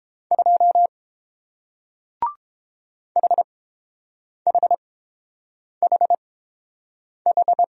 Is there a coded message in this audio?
Yes: Morse code from 0.3 to 7.6 s, heard as a keyed tone, about 31 wpm, spaced as for 7 wpm, 700 Hz. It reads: 2 HHHH